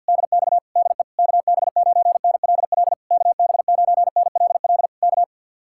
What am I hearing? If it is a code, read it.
Morse code, heard as a keyed tone, about 66 wpm, spaced as for 58 wpm, 700 Hz. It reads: DX DE KB9NLL KB9NLL K
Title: Speed-Racing Common QSO Phrases - 50wpm